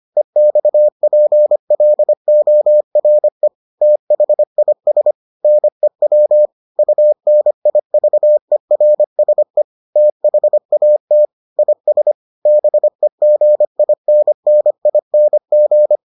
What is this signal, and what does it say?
Morse code, heard as a keyed tone, about 25 wpm, 600 Hz. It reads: EXPLORE THIS NEW UNIVERSE THAT IS BEGINNING